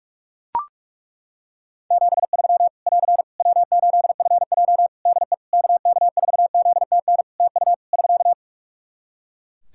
Morse code, heard as a keyed tone, about 42 wpm, 700 Hz. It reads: 73 <AR> W8FJ DE KK4ZTN TU <SK>
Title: Practice QSO - Exchange 4 - Ending - 45wpm